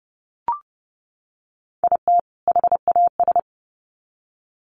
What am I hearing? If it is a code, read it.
Morse code, heard as a keyed tone, about 30 wpm, 700 Hz. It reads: IT HAS
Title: Sets of 2 Words 30wpm